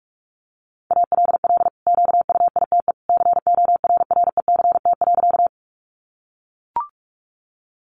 Morse code, heard as a keyed tone, about 44 wpm, 700 Hz. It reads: ALL QUITE CORRECT.
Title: Sentences from Top 700 Words 45wpm